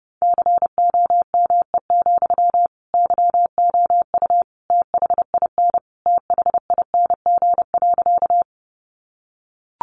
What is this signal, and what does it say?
Morse code, heard as a keyed tone, about 28 wpm, 700 Hz. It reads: COME, YOU THIN THING.
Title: Sentences from Top 1200 Words 30wpm